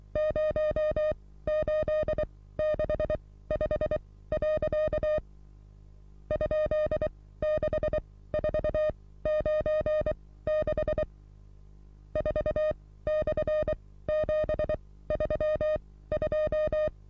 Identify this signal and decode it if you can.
Morse code, heard as a keyed tone, about 23 wpm, spaced as for 16 wpm, 610 Hz. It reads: 0865. ?6496 4/732
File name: repos / Morsekurs / DJ4UF Morsen27